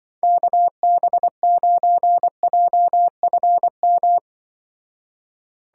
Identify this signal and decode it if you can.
Morse code, heard as a keyed tone, about 24 wpm, 700 Hz. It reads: KB9JFM
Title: Speed Racing - USA Call Signs 20wpm